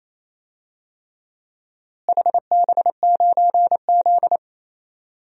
Morse code, heard as a keyed tone, about 28 wpm, 700 Hz. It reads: HB9Z
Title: DXCC-100 Call Signs - 4 Characters - Repeated 3x - 28wpm